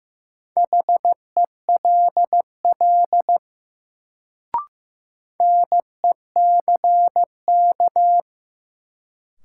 Morse code, heard as a keyed tone, about 15 wpm, 700 Hz. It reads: HELL NECK
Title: Sets of 5 Words Encore - New Word Drill - 15wpm